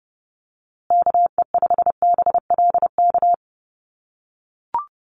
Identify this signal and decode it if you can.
Morse code, heard as a keyed tone, about 30 wpm, 700 Hz. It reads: KE5BLK
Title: USA Call Signs Repeated 2x 30wpm